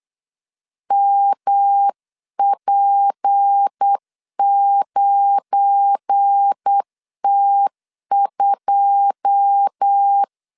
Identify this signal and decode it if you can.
Morse code, heard as a keyed tone, about 8 wpm, 800 Hz. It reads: MP9T2
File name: LEZIONI / 9 2 8wpm 3437